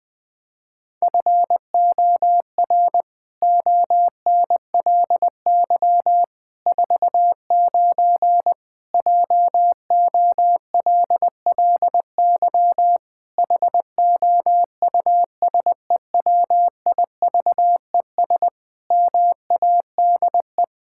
Morse code, heard as a keyed tone, about 20 wpm, 700 Hz. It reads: FOR ONLY 49 JOLLY HOUSEWIVES MADE
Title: Pangram Challenge v2 - 20wpm